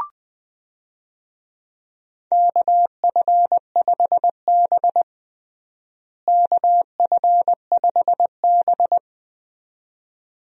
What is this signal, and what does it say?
Morse code, heard as a keyed tone, about 20 wpm, 700 Hz. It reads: KF5B KF5B
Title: USA Call Signs - 4 Characters - Repeated 2x - 20wpm